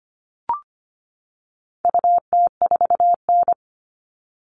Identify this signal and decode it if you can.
Morse code, heard as a keyed tone, about 25 wpm, 700 Hz. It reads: UT4N
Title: All Cut Numbers - 4-Digit 25wpm